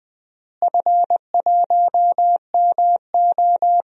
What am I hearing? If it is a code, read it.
Morse code, heard as a keyed tone, about 20 wpm, 700 Hz. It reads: F1MO
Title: DXCC-100 Call Signs - 4 Characters - Repeated 3x - 20wpm